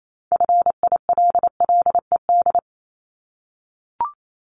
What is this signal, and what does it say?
Morse code, heard as a keyed tone, about 28 wpm, 700 Hz. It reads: FILLED
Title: Multi-letter Phonemes 28wpm